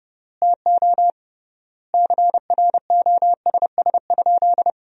Morse code, heard as a keyed tone, about 30 wpm, 700 Hz. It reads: TO CROSS?
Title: 3x Word-Spacing - Sentences from Top 300 Words - 30wpm